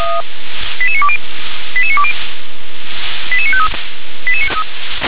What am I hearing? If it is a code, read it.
Five-tone selective call (ZVEI), heard as quick runs of five tones, tones of about 70 ms. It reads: 88920 88920 88953